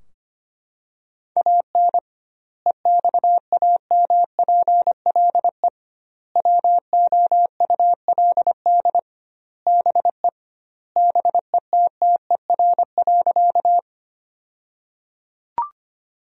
Morse code, heard as a keyed tone, about 25 wpm, 700 Hz. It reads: AN EXAMPLE WOULD BE BETTER.
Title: 2x Word-Spacing - Sentences from Top 400 Words - 25wpm